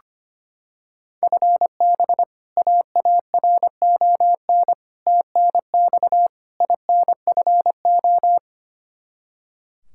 Morse code, heard as a keyed tone, about 25 wpm, 700 Hz. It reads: FB AARON TNX INFO
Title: Practice QSO - Exchange 3 - More Info - 25wpm